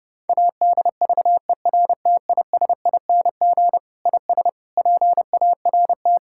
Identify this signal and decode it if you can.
Morse code, heard as a keyed tone, about 30 wpm, 700 Hz. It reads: ADVERTISING IS PART
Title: Sentences from Top 1000 Words - Review Part 6 - 30wpm